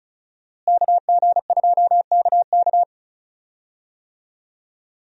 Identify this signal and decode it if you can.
Morse code, heard as a keyed tone, about 36 wpm, 700 Hz. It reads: KG2KK
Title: USA Call Signs - 5 Characters - Repeated 3x - 35wpm